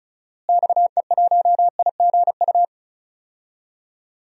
Morse code, heard as a keyed tone, about 34 wpm, 700 Hz. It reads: XE1IGU
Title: DXCC-100 Call Signs - 6 Characters - Repeated 3x - 35wpm